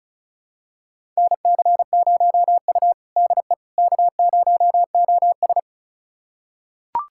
Morse code, heard as a keyed tone, about 36 wpm, 700 Hz. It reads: NC0U DE K0OS
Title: USA Callsign DE Callsign - 4 Characters - 35wpm